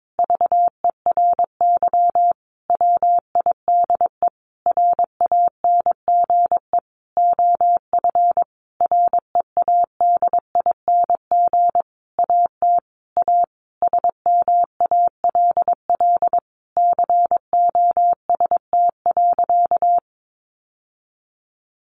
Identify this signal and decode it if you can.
Morse code, heard as a keyed tone, about 22 wpm, 700 Hz. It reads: VERY WIDE RANGE OF READING AT A SMALL COST.
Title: Sentences from Top 900 Words 22wpm